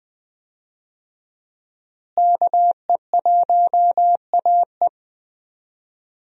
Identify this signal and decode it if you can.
Morse code, heard as a keyed tone, about 20 wpm, 700 Hz. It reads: KE1AE
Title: USA Call Signs - 5 Characters - Repeated 2x - 20wpm